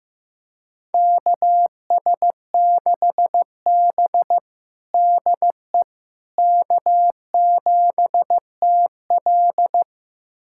Morse code, heard as a keyed tone, about 15 wpm, 700 Hz. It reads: KS6B DE K7TL
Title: USA Callsign DE Callsign - 4 Characters - 15wpm